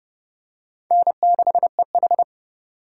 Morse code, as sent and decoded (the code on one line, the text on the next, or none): -. -.... ....
N6EH